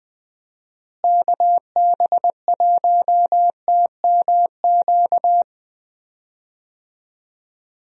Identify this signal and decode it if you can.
Morse code, heard as a keyed tone, about 20 wpm, 700 Hz. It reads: KB1TMQ
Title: USA Call Signs - 6 Characters - Repeated 2x - 20wpm